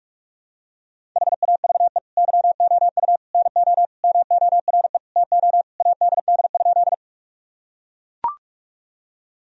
Morse code, heard as a keyed tone, about 43 wpm, 700 Hz. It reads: HAVE YOU NO MORE TO ADD?
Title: Sentences from Top 1000 Words - Review Part 1 - 45wpm